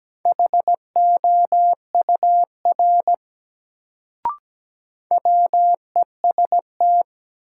Morse code, heard as a keyed tone, about 17 wpm, 700 Hz. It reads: HOUR WEST
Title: Sets of 5 Words - New Word Drill - 17wpm